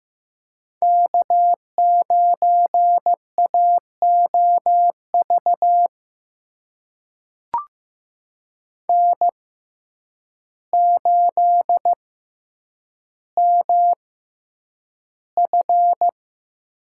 Morse code, heard as a keyed tone, about 15 wpm, spaced as for 8 wpm, 700 Hz. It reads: K9AOV N8MF
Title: USA Call Signs - 5 Characters - 6x Character Spacing - 15wpm